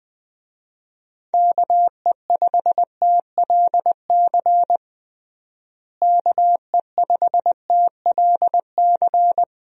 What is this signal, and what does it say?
Morse code, heard as a keyed tone, about 20 wpm, 700 Hz. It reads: KE5TLC KE5TLC
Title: USA Call Signs - 6 Characters - Repeated 3x - 20wpm